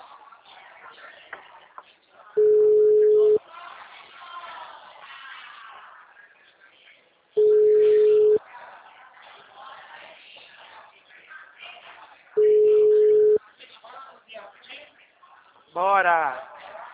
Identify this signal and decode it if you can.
Ringback tone (Europe, 425 Hz), with speech.